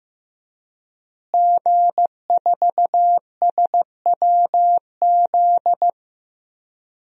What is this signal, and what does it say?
Morse code, heard as a keyed tone, about 15 wpm, 700 Hz. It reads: G4SWZ
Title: DXCC-100 Call Signs - 5 Characters - Repeated 3x - 15wpm